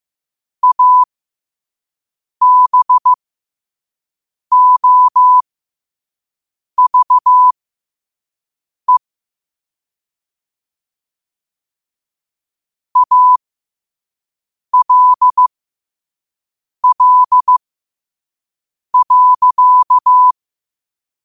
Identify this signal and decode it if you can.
Morse code, heard as a keyed tone, about 15 wpm, spaced as for 5 wpm, 1000 Hz. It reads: ABOVE ALL.
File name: qotdfiles / 5wpm